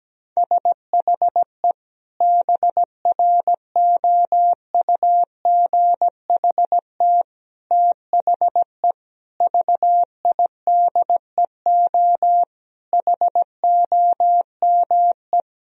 Morse code, heard as a keyed tone, about 17 wpm, 700 Hz. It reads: SHE BROUGHT THE VIDEO HOME
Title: Sentences from Top 1000 Words - Review Part 4 - 17wpm